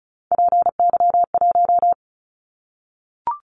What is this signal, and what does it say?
Morse code, heard as a keyed tone, about 36 wpm, 700 Hz. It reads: PY1
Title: Callsign Prefixes for Top 50 DXCC - 35wpm